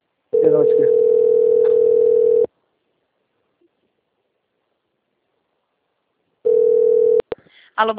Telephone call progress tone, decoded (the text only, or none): ringback tone